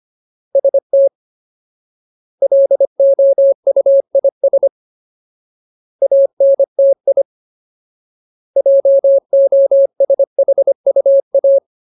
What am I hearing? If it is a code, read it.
Morse code, heard as a keyed tone, about 25 wpm, 550 Hz. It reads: ST LOUIS ANTI JOSHUA
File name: Ultimate Word List / 25WPM Ultimate